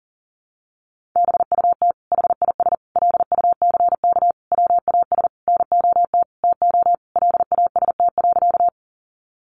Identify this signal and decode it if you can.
Morse code, heard as a keyed tone, about 40 wpm, 700 Hz. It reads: BUT HIS LUCK WAS NOT TO LAST.
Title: Sentences from Top 1500 Words 40wpm